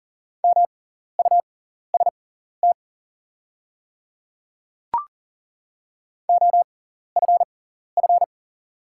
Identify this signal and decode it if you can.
Morse code, heard as a keyed tone, about 39 wpm, spaced as for 14 wpm, 700 Hz. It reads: MUST OFF
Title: 6x Character Spacing - Top 200 Words - 40wpm